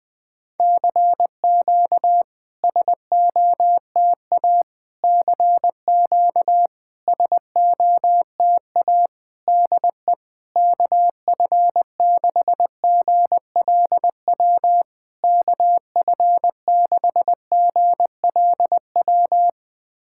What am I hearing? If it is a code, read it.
Morse code, heard as a keyed tone, about 20 wpm, 700 Hz. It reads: CQ SOTA CQ SOTA DE KF6GLW KF6GLW
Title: SOTA QSO Practice 20wpm